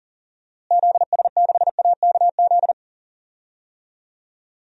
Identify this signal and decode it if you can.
Morse code, heard as a keyed tone, about 37 wpm, 700 Hz. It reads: ZS6AKZ
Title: DXCC-100 Call Signs - 6 Characters - Repeated 3x - 40wpm